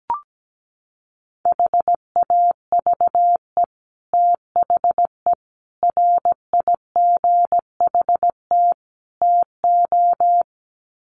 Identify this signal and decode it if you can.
Morse code, heard as a keyed tone, about 17 wpm, 700 Hz. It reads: HAVE THE RIGHT TO
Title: Sets of 4 Words 17wpm